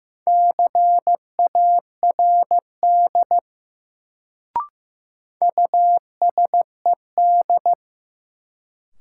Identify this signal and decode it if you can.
Morse code, heard as a keyed tone, about 15 wpm, 700 Hz. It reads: CARD USED
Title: Sentences from Top 600 Words - New Word Drill - 15wpm